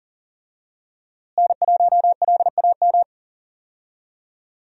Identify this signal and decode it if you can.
Morse code, heard as a keyed tone, about 40 wpm, 700 Hz. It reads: N1LAM
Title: USA Call Signs - 5 Characters - Repeated 3x - 40wpm